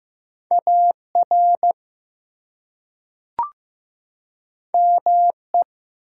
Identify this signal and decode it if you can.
Morse code, heard as a keyed tone, about 15 wpm, 700 Hz. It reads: AR ME
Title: US States and Canadian Provinces 15wpm